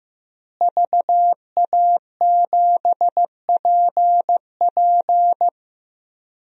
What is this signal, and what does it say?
Morse code, heard as a keyed tone, about 16 wpm, 700 Hz. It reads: VA7PP
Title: DXCC-100 Call Signs - 5 Characters - Repeated 3x - 15wpm